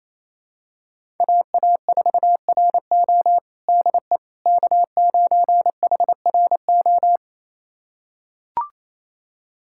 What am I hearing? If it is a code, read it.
Morse code, heard as a keyed tone, about 27 wpm, 700 Hz. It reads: AA4RO DE K9HRO
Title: USA Callsign DE Callsign - 5 Characters - 28wpm